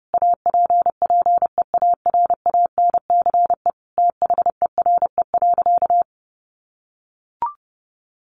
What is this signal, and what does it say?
Morse code, heard as a keyed tone, about 30 wpm, 700 Hz. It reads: APPEARANCE THERE.
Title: Sentences from Top 1100 Words 30wpm